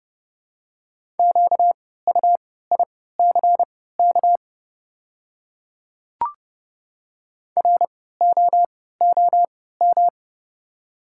Morse code, heard as a keyed tone, about 30 wpm, spaced as for 17 wpm, 700 Hz. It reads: QUICK ROOM
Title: 3x Character Spacing - Top 500 Words - 30wpm